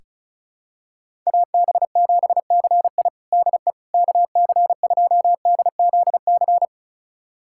Morse code, heard as a keyed tone, about 36 wpm, 700 Hz. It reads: AB7CI DE KC2DZC
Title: USA Callsign DE Callsign 35wpm